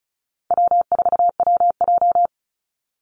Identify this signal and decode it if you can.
Morse code, heard as a keyed tone, about 34 wpm, 700 Hz. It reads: W4WJ
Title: Call Signs Repeated 2x - 35wpm